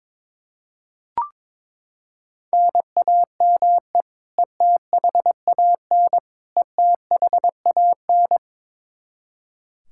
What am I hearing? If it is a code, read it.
Morse code, heard as a keyed tone, about 22 wpm, 700 Hz. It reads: NAME ETHAN ETHAN
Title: Practice QSO - Exchange 1 - Beginning - 22wpm